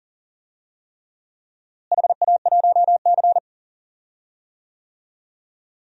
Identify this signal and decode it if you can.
Morse code, heard as a keyed tone, about 37 wpm, 700 Hz. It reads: HA1C
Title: DXCC-100 Call Signs - 4 Characters - Repeated 3x - 40wpm